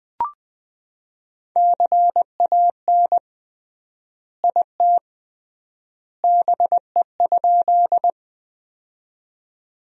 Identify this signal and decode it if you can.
Morse code, heard as a keyed tone, about 20 wpm, 700 Hz. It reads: CAN IT BE?
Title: 3x Word-Spacing - Sentences from Top 100 Words - 20wpm